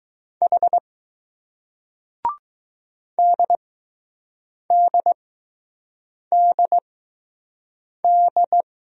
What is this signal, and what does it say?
Morse code, heard as a keyed tone, about 20 wpm, spaced as for 6 wpm, 700 Hz. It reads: H DDDD